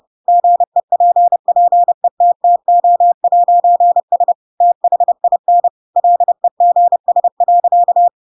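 Morse code, heard as a keyed tone, about 30 wpm, 700 Hz. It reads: GEPPETTO'S THIN LEGS.
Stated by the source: Chapter 2 of The Adventures of Pinocchio read as a book in Morse code (CW) at 30 wpm, by ik1hge.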